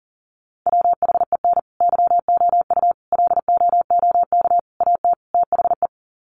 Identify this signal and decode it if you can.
Morse code, heard as a keyed tone, about 40 wpm, 700 Hz. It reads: WHEN YOU LOOK AT THE
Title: Sets of 5 Words 40wpm